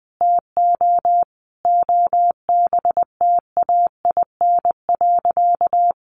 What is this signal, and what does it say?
Morse code, heard as a keyed tone, about 20 wpm, 700 Hz. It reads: TO OBTAIN.
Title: Sentences from Top 1300 Words 20wpm